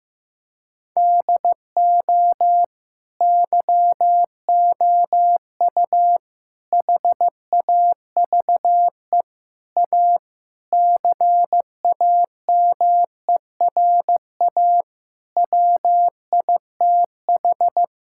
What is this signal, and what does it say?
Morse code, heard as a keyed tone, about 16 wpm, 700 Hz. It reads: DO YOU HAVE A CAMERA WITH
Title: Sentences from Top 1000 Words - Review Part 5 - 15wpm